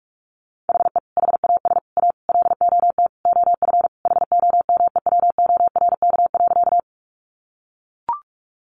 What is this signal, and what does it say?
Morse code, heard as a keyed tone, about 44 wpm, 700 Hz. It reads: HE HAS A LOT OF HOMEWORK.
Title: Additional Easy Phrases 45wpm